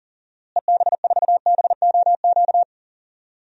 Morse code, heard as a keyed tone, about 40 wpm, 700 Hz. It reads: EB4BOQ
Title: DXCC-100 Call Signs - 6 Characters - Repeated 3x - 40wpm